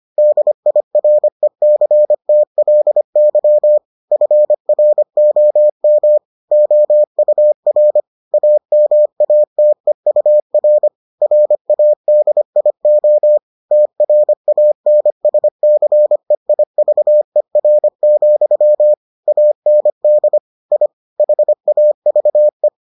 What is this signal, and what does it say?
Morse code, heard as a keyed tone, about 25 wpm, 600 Hz. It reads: DIRECTLY FROM OUR AMATEUR RADIO TRANSCEIVER, AND I HAVE